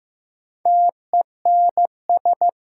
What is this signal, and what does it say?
Morse code, heard as a keyed tone, about 15 wpm, 700 Hz. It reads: TENS
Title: Sets of 4 Words - New Word Drill - 15wpm